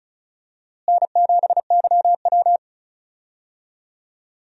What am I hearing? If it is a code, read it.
Morse code, heard as a keyed tone, about 36 wpm, 700 Hz. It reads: N7YW
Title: USA Call Signs - 4 Characters - Repeated 3x - 35wpm